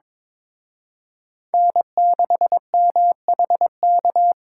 Morse code, heard as a keyed tone, about 26 wpm, 700 Hz. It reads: N6MHK
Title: Speed Racing - USA 1x3 Call Signs 22wpm